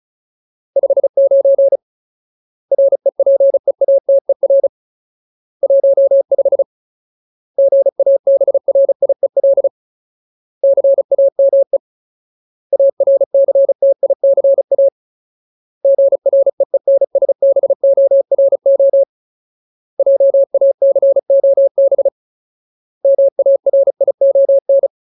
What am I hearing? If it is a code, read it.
Morse code, heard as a keyed tone, about 34 wpm, 550 Hz. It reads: 59 REPEATER 15 GABRIEL CAME ARCTICA GREENSBORO JACOB MARION
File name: Ultimate Word List / 35WPM Ultimate